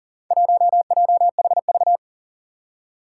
Morse code, heard as a keyed tone, about 40 wpm, 700 Hz. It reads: 1JHV
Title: Four Challenging Letter-Number Combinations 40wpm